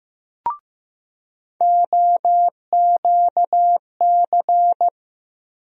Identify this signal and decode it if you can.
Morse code, heard as a keyed tone, about 15 wpm, 700 Hz. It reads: OQC
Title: USA Call Sign Postfixes - 3 Letters 15wpm